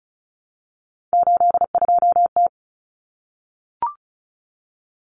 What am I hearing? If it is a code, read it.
Morse code, heard as a keyed tone, about 34 wpm, 700 Hz. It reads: 82T
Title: Common Cut Numbers - 3-Digit 35wpm